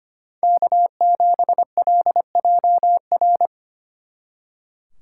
Morse code, heard as a keyed tone, about 31 wpm, 700 Hz. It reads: K7LJR
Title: Speed Racing - USA 1x3 Call Signs 25wpm